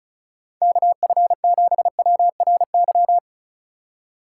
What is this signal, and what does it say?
Morse code, heard as a keyed tone, about 36 wpm, 700 Hz. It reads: KF7WRY
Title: USA Call Signs - 6 Characters - 3x Character Spacing - 35wpm